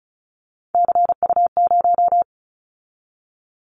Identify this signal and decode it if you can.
Morse code, heard as a keyed tone, about 34 wpm, 700 Hz. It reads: CU0
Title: Callsign Prefixes for Top 100 DXCC - 35wpm